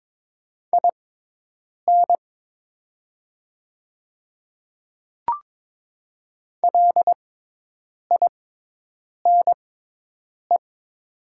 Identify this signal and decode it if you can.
Morse code, heard as a keyed tone, about 22 wpm, spaced as for 8 wpm, 700 Hz. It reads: IN LINE
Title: 6x Character Spacing - Top 200 Words - 22wpm